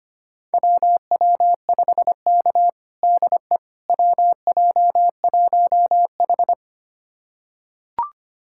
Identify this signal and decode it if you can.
Morse code, heard as a keyed tone, about 25 wpm, 700 Hz. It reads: WW5K DE WJ1H
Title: USA Callsign DE Callsign - 4 Characters - 25wpm